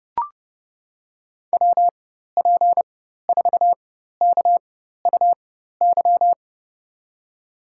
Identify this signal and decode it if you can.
Morse code, heard as a keyed tone, about 30 wpm, spaced as for 14 wpm, 700 Hz. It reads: WP4KUY
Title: USA Call Signs - 6 Characters - 4x Character Spacing - 30wpm